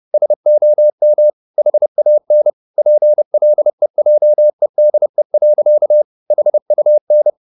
Morse code, heard as a keyed tone, about 30 wpm, 600 Hz. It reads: SOM HAN PLEJEDE. HUN
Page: Ferie 30 wpm